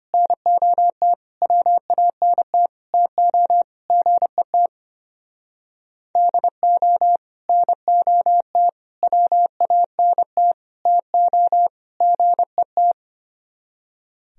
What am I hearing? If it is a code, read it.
Morse code, heard as a keyed tone, about 31 wpm, 700 Hz. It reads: NOT WANT TO GET DO NOT WANT TO GET